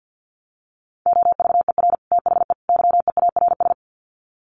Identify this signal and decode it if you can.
Morse code, heard as a keyed tone, about 50 wpm, 700 Hz. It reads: OVER THE YEARS
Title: Sets of 3 Words 50wpm